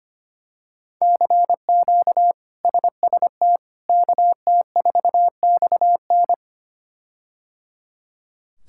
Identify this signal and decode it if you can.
Morse code, heard as a keyed tone, about 25 wpm, 700 Hz. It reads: CQ SST KT4XN
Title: K1USN SST QSO Practice - 25wpm